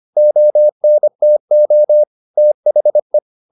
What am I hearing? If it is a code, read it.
Morse code, heard as a keyed tone, about 25 wpm, 600 Hz. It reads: ONTO THE